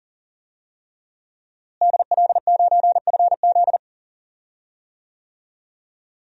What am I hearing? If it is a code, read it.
Morse code, heard as a keyed tone, about 37 wpm, 700 Hz. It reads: DL9FZ
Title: DXCC-100 Call Signs - 5 Characters - Repeated 3x - 40wpm